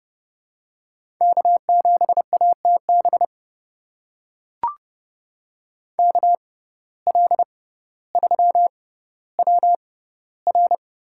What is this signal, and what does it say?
Morse code, heard as a keyed tone, about 30 wpm, spaced as for 15 wpm, 700 Hz. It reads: K7ATB KL3WR
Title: USA Call Signs - 5 Characters - 6x Character Spacing - 30wpm